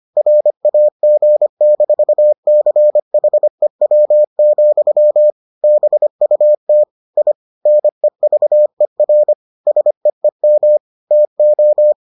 Morse code, heard as a keyed tone, about 25 wpm, 600 Hz. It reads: RAG-CHEW, BUT I NEVER SEEM TO